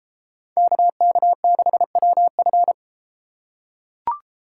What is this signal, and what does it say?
Morse code, heard as a keyed tone, about 30 wpm, 700 Hz. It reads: KK6WF